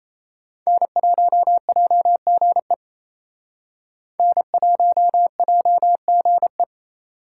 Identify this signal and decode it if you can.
Morse code, heard as a keyed tone, about 36 wpm, 700 Hz. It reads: N1JGE N1JGE